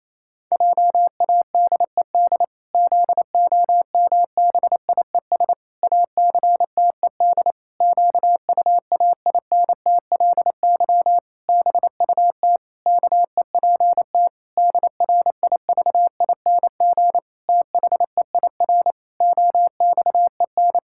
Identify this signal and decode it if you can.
Morse code, heard as a keyed tone, about 28 wpm, 700 Hz. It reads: JADED ZOMBIES ACTED QUAINTLY BUT KEPT DRIVING THEIR OXEN
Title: Pangram Challenge v2 - 28wpm